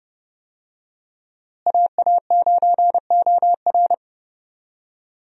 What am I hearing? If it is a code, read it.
Morse code, heard as a keyed tone, about 30 wpm, 700 Hz. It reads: AA9OR
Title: USA Call Signs - 5 Characters - Repeated 3x - 30wpm